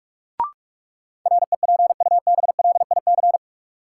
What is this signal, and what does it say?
Morse code, heard as a keyed tone, about 46 wpm, 700 Hz. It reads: REPUBLIC
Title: Sentences from Top 1500 Words - New Word Drill Alt - 45wpm